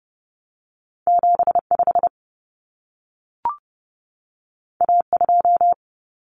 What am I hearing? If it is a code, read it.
Morse code, heard as a keyed tone, about 30 wpm, 700 Hz. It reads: 75 A2
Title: Common Cut Numbers - 2-Digit 30wpm